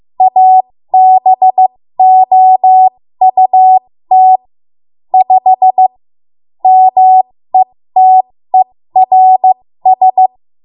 Morse code, heard as a keyed tone, about 15 wpm, spaced as for 13 wpm, 750 Hz. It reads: ABOUT 5 METERS
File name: Morse / 13WPM